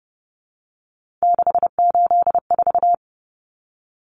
Morse code, heard as a keyed tone, about 30 wpm, 700 Hz. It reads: 684
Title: Numbers - 3-Digit 30wpm